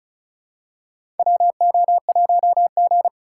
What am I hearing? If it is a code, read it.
Morse code, heard as a keyed tone, about 36 wpm, 700 Hz. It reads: WO1G
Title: USA Call Signs - 2x Character Spacing - 35wpm